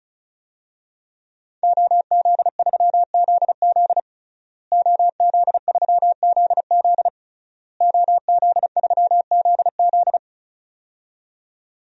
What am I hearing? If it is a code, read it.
Morse code, heard as a keyed tone, about 36 wpm, 700 Hz. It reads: OZ3ZZ OZ3ZZ OZ3ZZ
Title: DXCC-100 Call Signs - 5 Characters - Repeated 3x - 35wpm